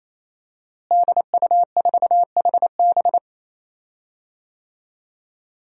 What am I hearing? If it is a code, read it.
Morse code, heard as a keyed tone, about 27 wpm, 700 Hz. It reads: DU4HB
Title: DXCC-100 Call Signs - 5 Characters - Repeated 3x - 28wpm